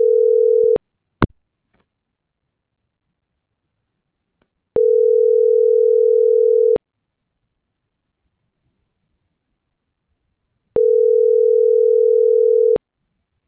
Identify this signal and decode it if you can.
Ringback tone (North America), 440 + 480 Hz.